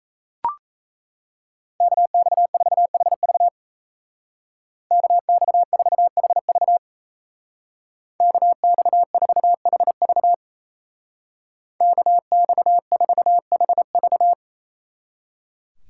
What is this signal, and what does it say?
Morse code, heard as a keyed tone, about 34 wpm, 700 Hz. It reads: KX4HV KX4HV KX4HV KX4HV
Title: Speed Racing - USA 2x2 Call Signs 28wpm